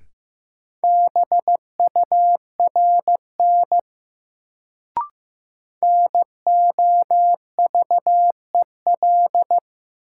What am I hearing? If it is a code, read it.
Morse code, heard as a keyed tone, about 15 wpm, 700 Hz. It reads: BURN NOVEL
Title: Sentences from Top 1300 Words - New Word Drill - 15wpm